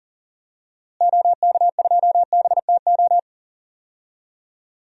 Morse code, heard as a keyed tone, about 39 wpm, 700 Hz. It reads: OK2BTO
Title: DXCC-100 Call Signs - 6 Characters - Repeated 2x - 40wpm